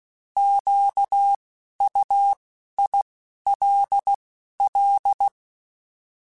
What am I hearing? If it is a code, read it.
Morse code, heard as a keyed tone, about 16 wpm, 800 Hz. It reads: Q U I L L
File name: Alphabetical / Alphabetical mp3s / Qs @10